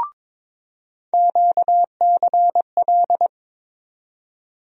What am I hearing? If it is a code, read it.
Morse code, heard as a keyed tone, about 22 wpm, 700 Hz. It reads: QCL
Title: USA Call Sign Postfixes - 3 Letters 22wpm